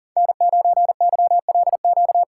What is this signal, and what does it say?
Morse code, heard as a keyed tone, about 40 wpm, 700 Hz. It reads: N9YLQ
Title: USA Call Signs - 5 Characters - 2x Character Spacing - 40wpm